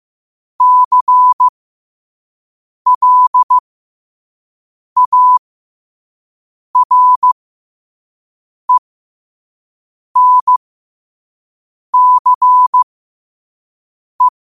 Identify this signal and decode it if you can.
Morse code, heard as a keyed tone, about 15 wpm, spaced as for 5 wpm, 1000 Hz. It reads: CLARENCE
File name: qotdfiles / 5wpm